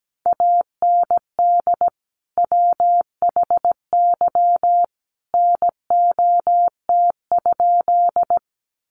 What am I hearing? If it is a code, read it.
Morse code, heard as a keyed tone, about 17 wpm, 700 Hz. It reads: AND WHY NOT?
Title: Sentences from Top 600 Words 17wpm